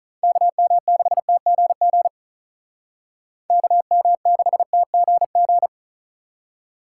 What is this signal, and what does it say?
Morse code, heard as a keyed tone, about 45 wpm, 700 Hz. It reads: KM6TGG KM6TGG